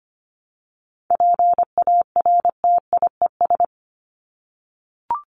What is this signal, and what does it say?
Morse code, heard as a keyed tone, about 25 wpm, 700 Hz. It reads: PARTIES
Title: ES:S Suffix - Top 100 Words with Base Words 25wpm